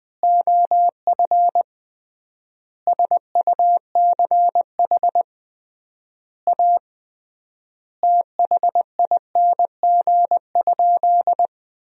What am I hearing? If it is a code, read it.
Morse code, heard as a keyed tone, about 20 wpm, 700 Hz. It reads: OF SUCH A THING?
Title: 3x Word-Spacing - Sentences from Top 200 Words - 20wpm